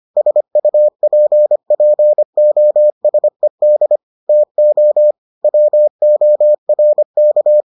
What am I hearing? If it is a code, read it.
Morse code, heard as a keyed tone, about 25 wpm, 600 Hz. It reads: SUPPOSED TO WORK